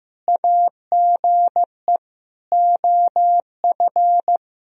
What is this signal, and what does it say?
Morse code, heard as a keyed tone, about 15 wpm, 700 Hz. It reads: AGE OF
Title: Sentences from Top 1000 Words - Review Part 3 - 15wpm